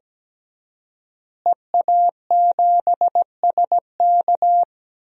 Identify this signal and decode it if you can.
Morse code, heard as a keyed tone, about 17 wpm, 700 Hz. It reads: EA7SK
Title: DXCC-100 Call Signs - 5 Characters - Repeated 3x - 17wpm